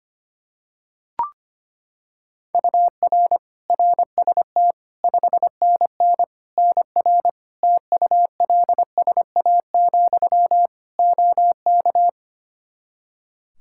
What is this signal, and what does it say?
Morse code, heard as a keyed tone, about 25 wpm, 700 Hz. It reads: UR RST 5NN NR TULSA, OK
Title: Practice QSO - Exchange 1 - Beginning - 25wpm